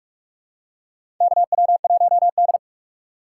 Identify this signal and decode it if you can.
Morse code, heard as a keyed tone, about 42 wpm, 700 Hz. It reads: KW1D
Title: USA Call Signs - 4 Characters - 4x Character Spacing - 45wpm